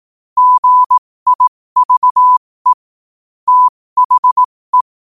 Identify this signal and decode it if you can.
Morse code, heard as a keyed tone, about 18 wpm, spaced as for 15 wpm, 1000 Hz. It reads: GIVE THE
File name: qotdfiles / 15wpm